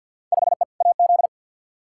Morse code, heard as a keyed tone, about 47 wpm, 700 Hz. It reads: 5E AZ
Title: Field Day Exchanges - 50wpm